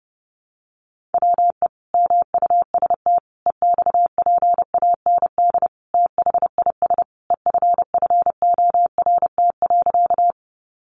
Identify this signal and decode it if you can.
Morse code, heard as a keyed tone, about 28 wpm, 700 Hz. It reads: WE MUST EXPAND THIS EFFORT.
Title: Sentences from Top 1400 Words 30wpm